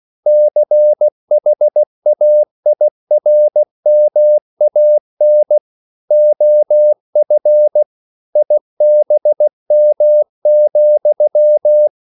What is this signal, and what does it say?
Morse code, heard as a keyed tone, about 16 wpm, 600 Hz. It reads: CHAIRMAN OF IBM,
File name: CW / Practice / EnglishQuotations0416